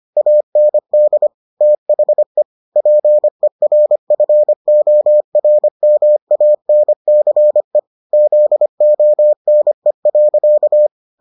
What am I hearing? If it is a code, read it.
Morse code, heard as a keyed tone, about 25 wpm, 600 Hz. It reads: AND THE PERFORMANCE ZONE.